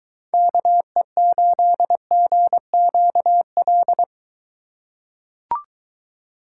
Morse code, heard as a keyed tone, about 22 wpm, 700 Hz. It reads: KE8GQL